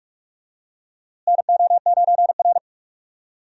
Morse code, heard as a keyed tone, about 46 wpm, 700 Hz. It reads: NO9R
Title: USA Call Signs - 4 Characters - Repeated 2x - 45wpm